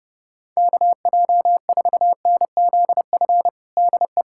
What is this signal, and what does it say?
Morse code, heard as a keyed tone, about 32 wpm, 700 Hz. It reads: KJ4NZF DE
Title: USA Callsign DE Callsign - 6 Characters - 30wpm